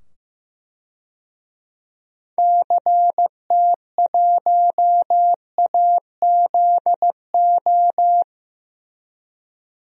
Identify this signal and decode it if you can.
Morse code, heard as a keyed tone, about 16 wpm, 700 Hz. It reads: CT1AZO
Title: DXCC-100 Call Signs - 6 Characters - 15wpm